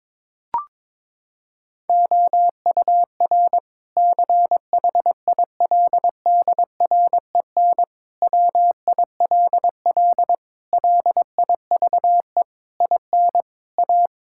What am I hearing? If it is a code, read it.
Morse code, heard as a keyed tone, about 21 wpm, 700 Hz. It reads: OUR CHILDREN WILL LIVE IN A
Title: Sentences from Top 1000 Words - Review Part 5 - 22wpm